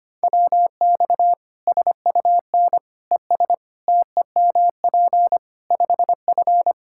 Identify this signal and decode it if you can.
Morse code, heard as a keyed tone, about 25 wpm, 700 Hz. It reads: WX SUN ES TEMP 5F
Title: Practice QSO - Exchange 2 - Info - 25wpm